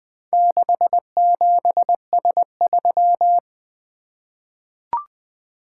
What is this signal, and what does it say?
Morse code, heard as a keyed tone, about 20 wpm, 700 Hz. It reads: 67S3
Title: Four Letter-Number Combinations - 20wpm